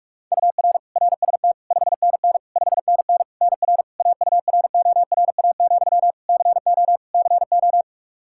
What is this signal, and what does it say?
Morse code, heard as a keyed tone, about 45 wpm, 700 Hz. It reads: UR RST 5NN 5NN NR AURORA, CO CO
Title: Practice QSO - Exchange 1 - Beginning - 45wpm